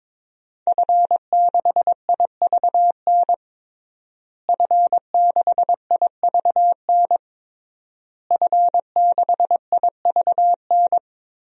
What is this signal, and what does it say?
Morse code, heard as a keyed tone, about 22 wpm, 700 Hz. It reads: F6IVN F6IVN F6IVN
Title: DXCC-100 Call Signs - 5 Characters - Repeated 3x - 22wpm